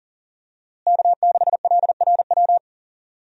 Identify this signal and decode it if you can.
Morse code, heard as a keyed tone, about 40 wpm, 700 Hz. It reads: K6LRW
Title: USA Call Signs - 5 Characters - Repeated 2x - 40wpm